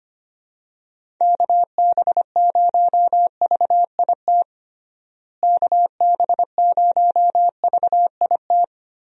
Morse code, heard as a keyed tone, about 25 wpm, 700 Hz. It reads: KB0VIT KB0VIT
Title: USA Call Signs - 6 Characters - Repeated 3x - 25wpm